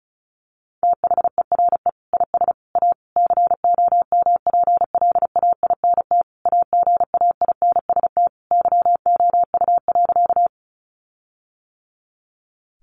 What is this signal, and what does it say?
Morse code, heard as a keyed tone, about 34 wpm, 700 Hz. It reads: THERE IS A COMPLAINT AGAINST YOU.
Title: Sentences from Top 1400 Words 35wpm